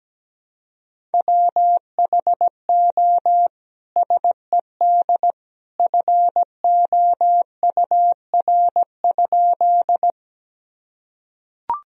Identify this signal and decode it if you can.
Morse code, heard as a keyed tone, about 17 wpm, 700 Hz. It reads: WHO SED FOUR?
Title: Sentences from Top 300 Words and Abbreviations 17wpm